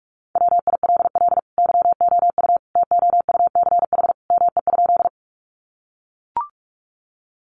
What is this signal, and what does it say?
Morse code, heard as a keyed tone, about 44 wpm, 700 Hz. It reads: WILL YOU TOUCH ME?
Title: Sentences from Top 1200 Words 45wpm